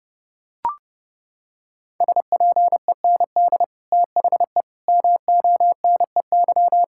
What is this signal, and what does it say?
Morse code, heard as a keyed tone, about 30 wpm, 700 Hz. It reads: SPEND THE MONEY
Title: Sentences from Top 1000 Words - Review Part 5 - 30wpm